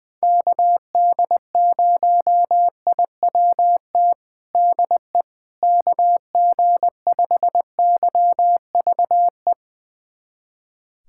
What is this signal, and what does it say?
Morse code, heard as a keyed tone, about 21 wpm, 700 Hz. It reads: KD0IWT DE KG5YVE
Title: USA Callsign DE Callsign - 6 Characters - 20wpm